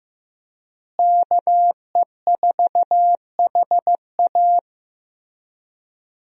Morse code, heard as a keyed tone, about 15 wpm, 700 Hz. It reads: KE4HA
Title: USA Call Signs - 5 Characters - Repeated 3x - 15wpm